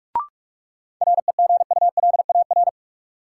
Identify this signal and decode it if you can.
Morse code, heard as a keyed tone, about 47 wpm, 700 Hz. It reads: REGULAR
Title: Sentences from Top 900 Words - New Word Drill - 45wpm